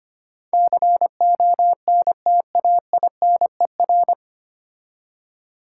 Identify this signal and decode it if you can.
Morse code, heard as a keyed tone, about 25 wpm, 700 Hz. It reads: CONTAINER
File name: practice / 25wpm / 9 Letter Words - First 3 Letters Given - 25wpm